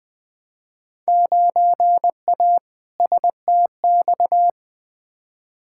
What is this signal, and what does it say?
Morse code, heard as a keyed tone, about 20 wpm, 700 Hz. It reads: 9A STX
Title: Field Day Exchanges - 20wpm